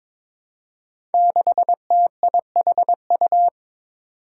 Morse code, heard as a keyed tone, about 22 wpm, 700 Hz. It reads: 6TIHU
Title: Five Challenging Letter-Number Combinations 22wpm